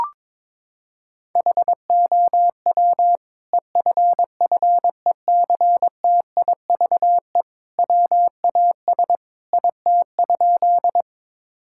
Morse code, heard as a keyed tone, about 22 wpm, 700 Hz. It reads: HOW EFFECTIVE WAS IT?
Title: Sentences from Top 1000 Words - Review Part 6 - 22wpm